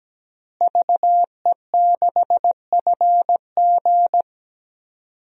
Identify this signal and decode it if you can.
Morse code, heard as a keyed tone, about 17 wpm, 700 Hz. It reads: VE6FG
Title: DXCC-100 Call Signs - 5 Characters - Repeated 3x - 17wpm